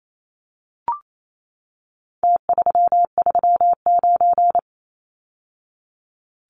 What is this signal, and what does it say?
Morse code, heard as a keyed tone, about 28 wpm, 700 Hz. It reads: T339
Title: Numbers - Contest Style - 4-Digit 28wpm